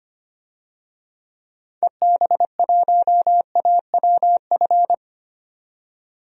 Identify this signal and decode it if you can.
Morse code, heard as a keyed tone, about 25 wpm, 700 Hz. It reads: EB1AWF
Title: DXCC-100 Call Signs - 6 Characters - 25wpm